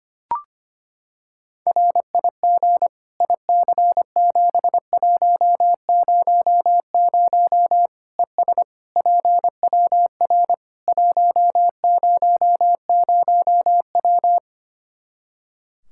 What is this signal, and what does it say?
Morse code, heard as a keyed tone, about 25 wpm, 700 Hz. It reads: RIG IC7100 ES PWR 100W
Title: Practice QSO - Exchange 2 - Info - 25wpm